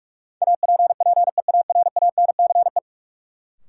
Morse code, heard as a keyed tone, about 46 wpm, 700 Hz. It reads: APPEARANCE
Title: Sentences from Top 1100 Words - New Word Drill - 45wpm